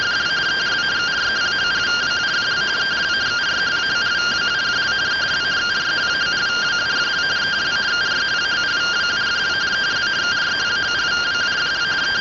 Two-tone teleprinter signal, about 170 Hz shift.
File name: findings / CT AXIAL -142.gen-17 rtty rx